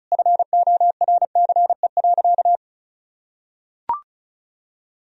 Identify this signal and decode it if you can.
Morse code, heard as a keyed tone, about 34 wpm, 700 Hz. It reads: FORCE.
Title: Sentences from Top 1000 Words - Review Part 4 - 35wpm